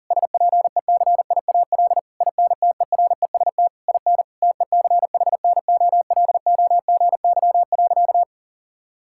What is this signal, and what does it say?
Morse code, heard as a keyed tone, about 37 wpm, 700 Hz. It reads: SPECIAL INTEREST IN TECHNOLOGY.
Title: Sentences from Top 1000 Words - Review Part 5 - 40wpm